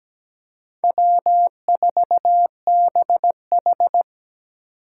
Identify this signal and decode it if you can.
Morse code, heard as a keyed tone, about 20 wpm, 700 Hz. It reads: W4BH
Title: Speed Racing - USA 1x2 Call Signs 17wpm